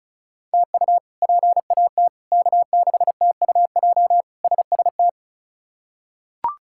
Morse code, heard as a keyed tone, about 36 wpm, 700 Hz. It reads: TU PAT K6TUJ SST
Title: K1USN SST QSO Practice - 35wpm